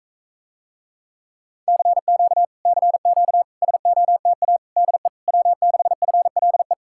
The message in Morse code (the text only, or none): CQ CQ SOTA DE W6FLE